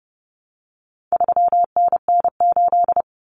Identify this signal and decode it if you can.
Morse code, heard as a keyed tone, about 30 wpm, 700 Hz. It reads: 3NN8
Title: Common Cut Numbers - 4-Digit 30wpm